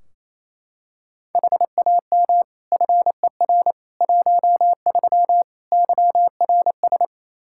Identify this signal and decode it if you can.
Morse code, heard as a keyed tone, about 28 wpm, 700 Hz. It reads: HAM FER 13 YRS
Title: Practice QSO - Exchange 3 - More Info - 28wpm